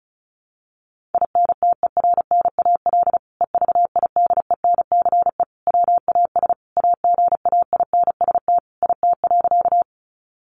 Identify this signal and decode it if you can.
Morse code, heard as a keyed tone, about 34 wpm, 700 Hz. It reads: INTERNAL EVIDENCE WAS AGAINST IT.
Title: Sentences from Top 1400 Words 35wpm